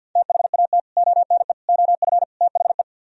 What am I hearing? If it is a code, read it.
Morse code, heard as a keyed tone, about 64 wpm, 700 Hz. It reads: THAT ONE OF THE